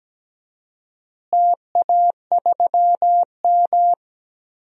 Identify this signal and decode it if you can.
Morse code, heard as a keyed tone, about 17 wpm, 700 Hz. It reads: TA3M
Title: DXCC-100 Call Signs - 4 Characters - Repeated 3x - 17wpm